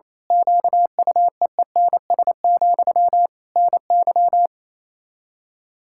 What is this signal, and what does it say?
Morse code, heard as a keyed tone, about 28 wpm, 700 Hz. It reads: QUEENS, NY
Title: Practice QSO - Exchange 1 - Beginning - 28wpm